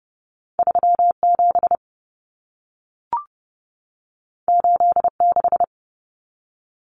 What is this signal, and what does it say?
Morse code, heard as a keyed tone, about 30 wpm, 700 Hz. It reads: 37 86
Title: Common Cut Numbers - 2-Digit 30wpm